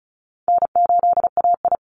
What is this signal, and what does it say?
Morse code, heard as a keyed tone, about 34 wpm, 700 Hz. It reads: N8AI
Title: Call Signs Repeated 2x - 35wpm